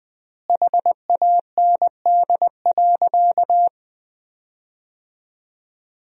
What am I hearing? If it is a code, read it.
Morse code, heard as a keyed tone, about 20 wpm, 700 Hz. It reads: HAND.
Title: 4x Word-Spacing - Sentences from Top 400 Words - 20wpm